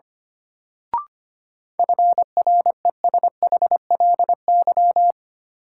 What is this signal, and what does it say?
Morse code, heard as a keyed tone, about 25 wpm, 700 Hz. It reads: FRESHLY
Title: Sets of 5 Words Encore - New Word Drill - 25wpm